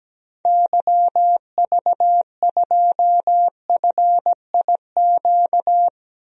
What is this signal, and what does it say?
Morse code, heard as a keyed tone, about 17 wpm, 700 Hz. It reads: YV2FIQ
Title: DXCC-100 Call Signs - 6 Characters - 17wpm